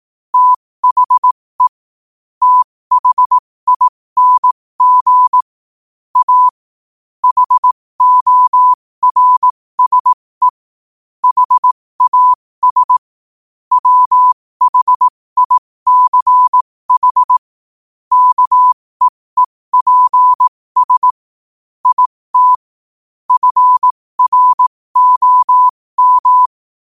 Morse code, heard as a keyed tone, about 18 wpm, spaced as for 15 wpm, 1000 Hz. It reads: THE THING A HORSE HAS WHICH KEEPS IT FROM
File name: qotdfiles / 15wpm